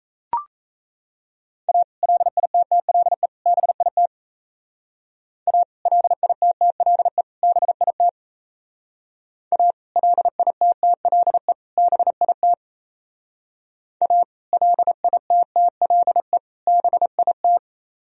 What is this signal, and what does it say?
Morse code, heard as a keyed tone, about 35 wpm, 700 Hz. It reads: A LITTLE BIT A LITTLE BIT A LITTLE BIT A LITTLE BIT